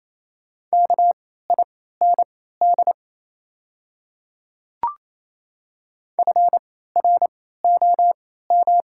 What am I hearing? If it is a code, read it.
Morse code, heard as a keyed tone, about 27 wpm, spaced as for 16 wpm, 700 Hz. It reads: KIND FROM
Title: 3x Character Spacing - Top 200 Words - 28wpm